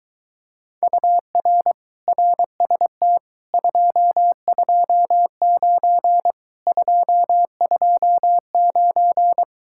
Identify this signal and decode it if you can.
Morse code, heard as a keyed tone, about 23 wpm, 700 Hz. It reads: UR RST 229 229